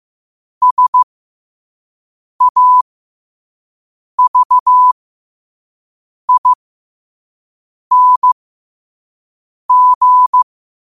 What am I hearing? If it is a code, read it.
Morse code, heard as a keyed tone, about 15 wpm, spaced as for 5 wpm, 1000 Hz. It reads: SAVING
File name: qotdfiles / 5wpm